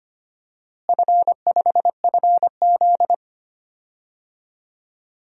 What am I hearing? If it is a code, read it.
Morse code, heard as a keyed tone, about 25 wpm, 700 Hz. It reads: F5FZ
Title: DXCC-100 Call Signs - 4 Characters - Repeated 2x - 25wpm